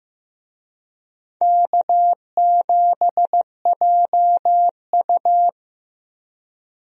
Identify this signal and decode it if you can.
Morse code, heard as a keyed tone, about 15 wpm, 700 Hz. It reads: K7JU
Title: USA Call Signs - 4 Characters - Repeated 2x - 15wpm